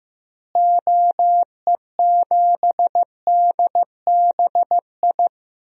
Morse code, heard as a keyed tone, about 15 wpm, 700 Hz. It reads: OE7DBI
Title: DXCC-100 Call Signs - 6 Characters - Repeated 3x - 15wpm